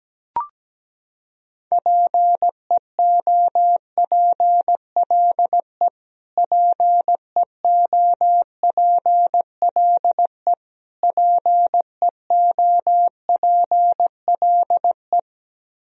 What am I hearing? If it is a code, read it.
Morse code, heard as a keyed tone, about 17 wpm, 700 Hz. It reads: PEOPLE PEOPLE PEOPLE
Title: Top 100 Words Repeated 17wpm